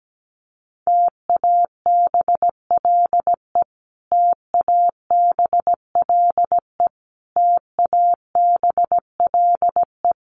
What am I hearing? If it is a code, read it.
Morse code, heard as a keyed tone, about 17 wpm, 700 Hz. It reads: TABLE TABLE TABLE
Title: Top 500 Words Repeated 17wpm